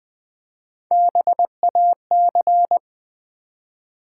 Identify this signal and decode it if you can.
Morse code, heard as a keyed tone, about 20 wpm, 700 Hz. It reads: BAC
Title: Build A Word - 3 Letters At A Time - 8-Letter Words - 20wpm